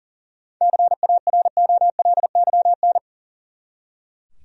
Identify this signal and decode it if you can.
Morse code, heard as a keyed tone, about 40 wpm, 700 Hz. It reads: CAROLYN
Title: USA Common Names - 40wpm